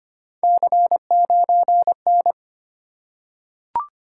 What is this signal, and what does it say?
Morse code, heard as a keyed tone, about 25 wpm, 700 Hz. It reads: C9N
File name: Three Letter-Number Combinations - 25wpm